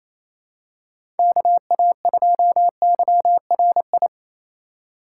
Morse code, heard as a keyed tone, about 28 wpm, 700 Hz. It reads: KA2YRI
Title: USA Call Signs - 6 Characters - 2x Character Spacing - 28wpm